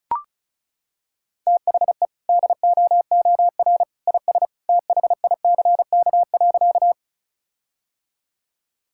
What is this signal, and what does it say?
Morse code, heard as a keyed tone, about 36 wpm, 700 Hz. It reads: THE DOOR IS THICK.
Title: Sentences from Top 1600 Words 35wpm